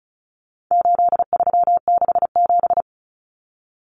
Morse code, heard as a keyed tone, about 34 wpm, 700 Hz. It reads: 8367
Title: Common Cut Numbers - 4-Digit 35wpm